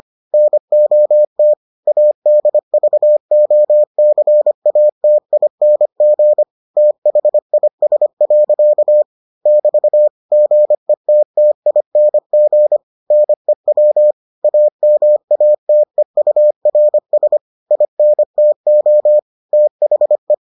Morse code, heard as a keyed tone, about 25 wpm, 600 Hz. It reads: NOT ADVOCATING THIS. <BT> GETTING NEW AMATEURS INTO THE